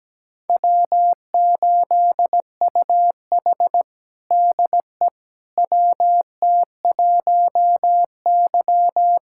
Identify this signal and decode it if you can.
Morse code, heard as a keyed tone, about 17 wpm, 700 Hz. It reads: W8UH DE WT1Y
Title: USA Callsign DE Callsign - 4 Characters - 17wpm